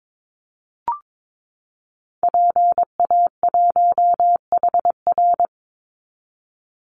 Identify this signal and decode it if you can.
Morse code, heard as a keyed tone, about 22 wpm, 700 Hz. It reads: PA1HR
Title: Call Signs Repeated 2x - 22wpm